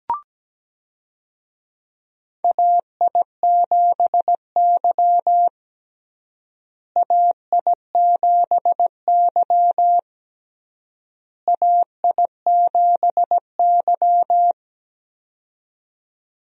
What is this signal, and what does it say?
Morse code, heard as a keyed tone, about 17 wpm, 700 Hz. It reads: AI7Y AI7Y AI7Y
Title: USA Call Signs - 4 Characters - Repeated 3x - 17wpm